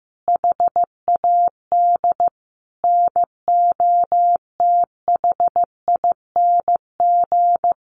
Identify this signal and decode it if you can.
Morse code, heard as a keyed tone, about 15 wpm, 700 Hz. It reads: HAD NOTHING
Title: Sentences from Top 600 Words 15wpm